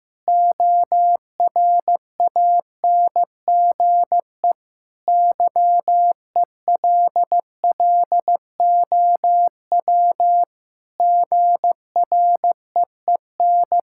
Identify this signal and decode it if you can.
Morse code, heard as a keyed tone, about 14 wpm, 700 Hz. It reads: ORANGE YELLOW GREEN
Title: Pangram Challenge v2 - 15wpm